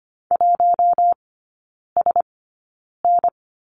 Morse code, heard as a keyed tone, about 25 wpm, spaced as for 9 wpm, 700 Hz. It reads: 1SN
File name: course / 25:10wpm / 093 Lesson 16e - Letter c - Three Characters 25wpm